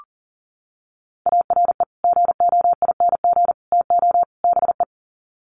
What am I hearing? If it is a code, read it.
Morse code, heard as a keyed tone, about 40 wpm, 700 Hz. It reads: ARE GOING TO BE
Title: Sets of 4 Words 40wpm